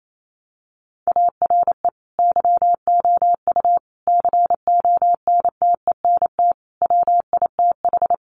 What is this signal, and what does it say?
Morse code, heard as a keyed tone, about 28 wpm, 700 Hz. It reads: ARE YOU CONTENT WITH
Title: Sentences from Top 900 Words 28wpm